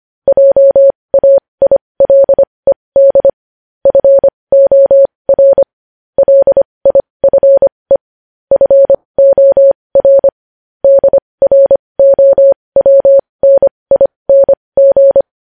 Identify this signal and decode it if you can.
Morse code, heard as a keyed tone, about 25 wpm, spaced as for 20 wpm, 550 Hz. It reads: JAILED FOR LIFE FOR DROWNING
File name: podcast / News headlines-1x-25x20